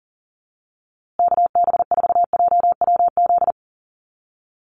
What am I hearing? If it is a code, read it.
Morse code, heard as a keyed tone, about 40 wpm, 700 Hz. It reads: KB4JWZ
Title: USA Call Signs Repeated 2x 40wpm